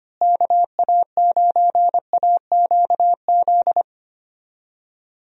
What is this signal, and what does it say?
Morse code, heard as a keyed tone, about 25 wpm, 700 Hz. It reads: KA9AQZ
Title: USA Call Signs - 6 Characters - Repeated 3x - 25wpm